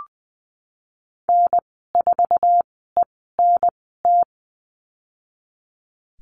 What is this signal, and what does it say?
Morse code, heard as a keyed tone, about 20 wpm, 700 Hz. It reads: N 4 E N T
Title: 055 Lesson 10g - Letter r - Call Signs 20wpm